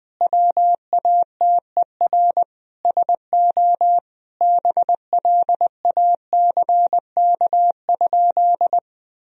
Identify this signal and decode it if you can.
Morse code, heard as a keyed tone, about 20 wpm, 700 Hz. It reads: WATER SO BLACK?
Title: Sentences from Top 1000 Words - Review Part 2 - 20wpm